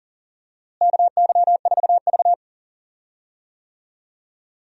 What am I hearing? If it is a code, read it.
Morse code, heard as a keyed tone, about 40 wpm, 700 Hz. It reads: KY4V
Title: USA Call Signs - 4 Characters - Repeated 2x - 40wpm